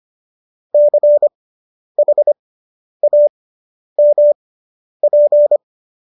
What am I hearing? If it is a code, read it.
Morse code, heard as a keyed tone, about 25 wpm, spaced as for 10 wpm, 590 Hz. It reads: CHAMP